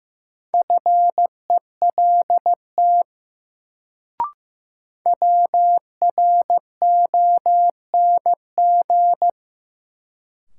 Morse code, heard as a keyed tone, about 15 wpm, 700 Hz. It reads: FELT WRONG
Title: Sets of 5 Words Encore - New Word Drill - 15wpm